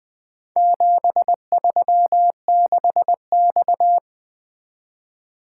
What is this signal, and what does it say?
Morse code, heard as a keyed tone, about 20 wpm, 700 Hz. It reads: 736X
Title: Four Letter-Number Combinations - 20wpm